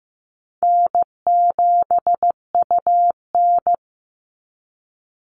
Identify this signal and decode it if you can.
Morse code, heard as a keyed tone, about 15 wpm, 700 Hz. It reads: N7UN
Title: NA SOTA Activator Callsigns 15wpm